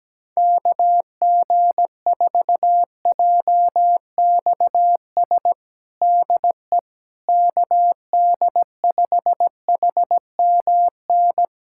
Morse code, heard as a keyed tone, about 17 wpm, 700 Hz. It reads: KG4JXS DE KD5HMN
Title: USA Callsign DE Callsign - 6 Characters - 17wpm